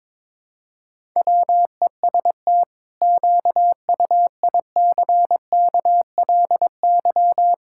Morse code, heard as a keyed tone, about 22 wpm, 700 Hz. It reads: WEST QUICKLY
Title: Pangram Challenge v2 - 22wpm